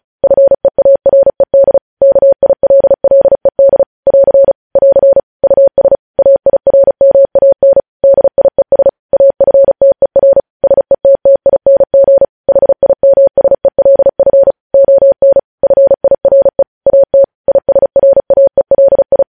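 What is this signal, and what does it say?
Morse code, heard as a keyed tone, about 36 wpm, 550 Hz. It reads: FEARED KILLED <AR> <AR> US AIRMAN DIES AFTER SETTING HIMSELF ON FIRE AT ISRAELI
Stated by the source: News headlines sent in Morse at 35 WPM.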